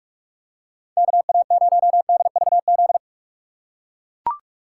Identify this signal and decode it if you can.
Morse code, heard as a keyed tone, about 43 wpm, 700 Hz. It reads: KA0DVZ
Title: USA Call Signs - 6 Characters - 6x Character Spacing - 45wpm